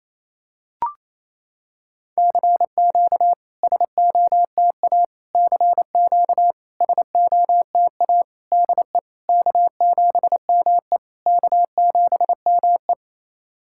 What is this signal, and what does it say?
Morse code, heard as a keyed tone, about 28 wpm, 690 Hz. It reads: CQ SOTA CQ SOTA DE K7ME K7ME
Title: SOTA QSO Practice 28wpm